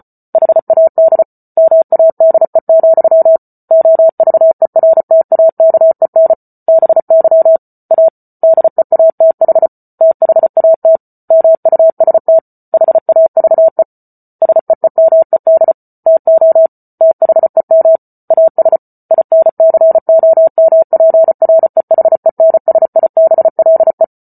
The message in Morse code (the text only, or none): HAD MADE, OVERTAKEN BY A DEATH THAT MUST HAVE SEEMED TO THEM AS INCOMPREHENSIBLE